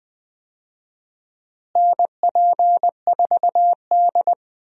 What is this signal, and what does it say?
Morse code, heard as a keyed tone, about 20 wpm, 700 Hz. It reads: NP4D
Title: USA Call Signs - 4 Characters - Repeated 3x - 20wpm